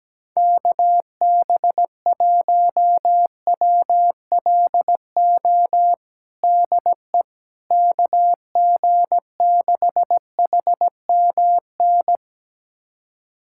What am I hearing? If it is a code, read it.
Morse code, heard as a keyed tone, about 17 wpm, 700 Hz. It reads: KB1WLO DE KG6HMN
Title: USA Callsign DE Callsign 17wpm